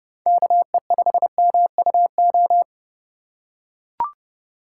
Morse code, heard as a keyed tone, about 29 wpm, 700 Hz. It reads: KE5MUO
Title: USA Call Signs - 6 Characters - 4x Character Spacing - 30wpm